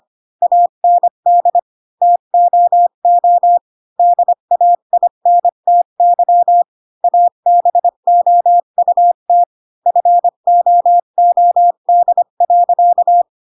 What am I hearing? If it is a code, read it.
Morse code, heard as a keyed tone, about 25 wpm, 700 Hz. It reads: AND TOO DAINTY ABOUT FOOD.
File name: Pinocchio / Pinocchio ch7 20 25 ik1hge